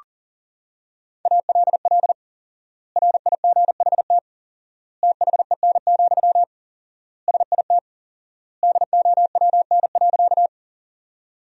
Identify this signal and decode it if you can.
Morse code, heard as a keyed tone, about 40 wpm, 700 Hz. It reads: ALL RIGHT THEN, SIT DOWN.
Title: 4x Word-Spacing - Sentences from Top 500 Words - 40wpm